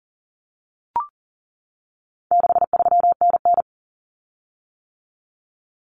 Morse code, heard as a keyed tone, about 40 wpm, 700 Hz. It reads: 63NN
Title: Common Cut Numbers - 4-Digit 40wpm